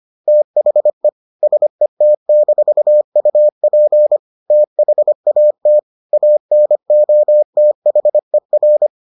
Morse code, heard as a keyed tone, about 25 wpm, 600 Hz. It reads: THE SET-UP THAT ANOTHER